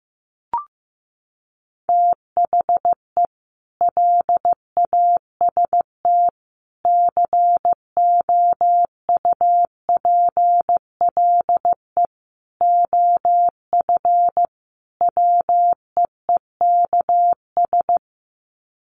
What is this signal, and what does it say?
Morse code, heard as a keyed tone, about 15 wpm, 700 Hz. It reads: THE LAST COUPLE OF WEEKS
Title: Sets of 5 Words Encore2 15wpm